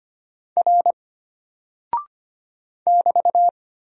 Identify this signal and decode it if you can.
Morse code, heard as a keyed tone, about 25 wpm, 700 Hz. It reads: R <BT>
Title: Single Character (Letter:Number:Punctuation:Prosign) Alt 25wpm